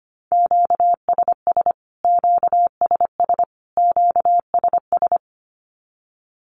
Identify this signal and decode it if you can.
Morse code, heard as a keyed tone, about 25 wpm, 700 Hz. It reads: QSS QSS QSS
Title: Q-Codes Repeated 3x 25wpm